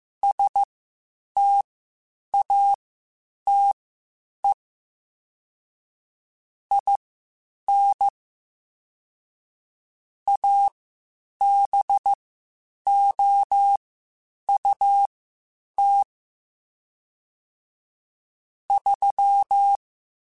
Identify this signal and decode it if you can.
Morse code, heard as a keyed tone, about 15 wpm, spaced as for 8 wpm, 800 Hz. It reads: STATE IN ABOUT 3